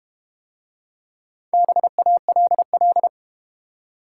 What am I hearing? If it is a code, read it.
Morse code, heard as a keyed tone, about 32 wpm, 700 Hz. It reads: BALL